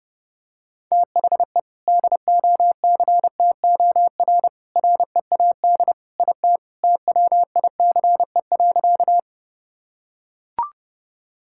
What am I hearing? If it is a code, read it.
Morse code, heard as a keyed tone, about 32 wpm, 700 Hz. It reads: THE DOCTOR READ IT TWICE.
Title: Sentences from Top 1600 Words 30wpm